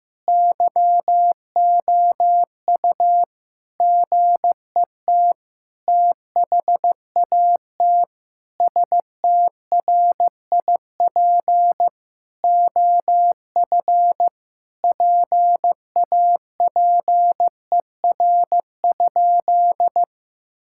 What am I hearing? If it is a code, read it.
Morse code, heard as a keyed tone, about 15 wpm, 700 Hz. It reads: YOU GET THAT STRIP OF PAPER?
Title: Sentences from Top 1600 Words 15wpm